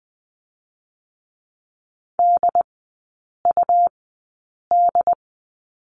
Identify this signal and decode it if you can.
Morse code, heard as a keyed tone, about 20 wpm, spaced as for 8 wpm, 700 Hz. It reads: DUD
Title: 100 Lesson 17e - Letter u - Three Characters 20wpm